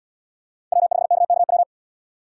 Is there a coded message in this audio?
Yes: Morse code from 0.7 to 1.6 s, heard as a keyed tone, about 71 wpm, 700 Hz. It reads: W5KKW